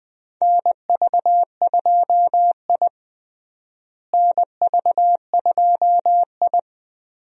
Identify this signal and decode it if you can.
Morse code, heard as a keyed tone, about 20 wpm, 700 Hz. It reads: NV2I NV2I
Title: USA Call Signs - 4 Characters - Repeated 2x - 20wpm